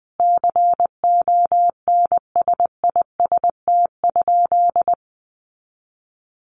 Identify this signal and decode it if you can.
Morse code, heard as a keyed tone, about 20 wpm, 700 Hz. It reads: CONSIST?
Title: Sentences from Top 1400 Words 20wpm